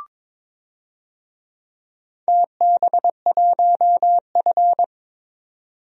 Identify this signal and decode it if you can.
Morse code, heard as a keyed tone, about 22 wpm, 700 Hz. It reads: TB1F
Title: DXCC-100 Call Signs - 4 Characters - 22wpm